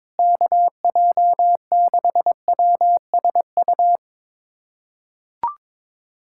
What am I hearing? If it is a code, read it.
Morse code, heard as a keyed tone, about 21 wpm, 700 Hz. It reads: KJ6WSU
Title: USA Callsign DE Callsign 22wpm